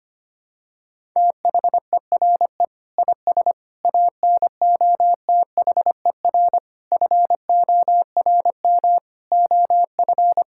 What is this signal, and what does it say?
Morse code, heard as a keyed tone, about 25 wpm, 700 Hz. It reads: THERE IS ANOTHER FORM OF
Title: Sentences from Top 1000 Words - Review Part 6 - 25wpm